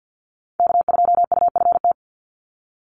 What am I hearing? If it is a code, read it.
Morse code, heard as a keyed tone, about 50 wpm, 700 Hz. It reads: K2UFT
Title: Call Signs - 50wpm